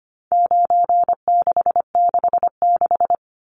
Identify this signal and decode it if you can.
Morse code, heard as a keyed tone, about 25 wpm, 700 Hz. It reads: 9666
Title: Numbers - 4-Digit 25wpm